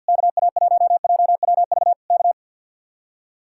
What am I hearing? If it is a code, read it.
Morse code, heard as a keyed tone, about 62 wpm, 700 Hz. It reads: KA1JWV K
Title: Speed-Racing Common QSO Phrases Encore - 50wpm